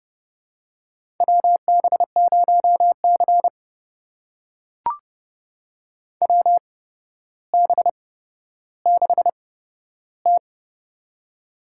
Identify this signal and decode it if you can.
Morse code, heard as a keyed tone, about 30 wpm, spaced as for 13 wpm, 700 Hz. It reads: WB0C WB6T
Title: USA Call Signs - 4 Characters - 8x Character Spacing - 30wpm